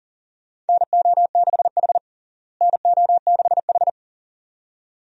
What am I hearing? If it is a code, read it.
Morse code, heard as a keyed tone, about 40 wpm, 700 Hz. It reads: NO6H NO6H
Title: USA Call Signs - 4 Characters - Repeated 2x - 40wpm